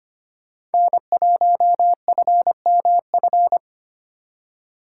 Morse code, heard as a keyed tone, about 33 wpm, 700 Hz. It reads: N1FMF